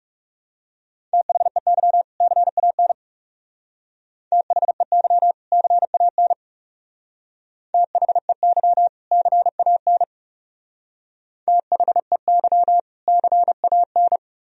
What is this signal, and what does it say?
Morse code, heard as a keyed tone, about 39 wpm, 700 Hz. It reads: THEY CAN THEY CAN THEY CAN THEY CAN